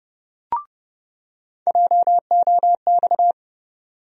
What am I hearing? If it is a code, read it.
Morse code, heard as a keyed tone, about 30 wpm, 700 Hz. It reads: JOX
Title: USA Call Sign Postfixes - 3 Letters 30wpm